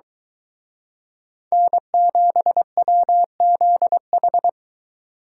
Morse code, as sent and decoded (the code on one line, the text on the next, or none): -. --... .-- --.. ....
N7WZH